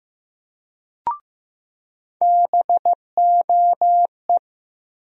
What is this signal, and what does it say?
Morse code, heard as a keyed tone, about 16 wpm, 700 Hz. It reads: BOE
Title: USA Call Sign Postfixes - 3 Letters 15wpm